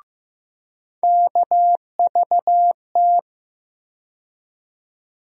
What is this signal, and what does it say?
Morse code, heard as a keyed tone, about 15 wpm, 700 Hz. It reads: KVT
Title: USA Call Sign Postfixes - 3 Letters 15wpm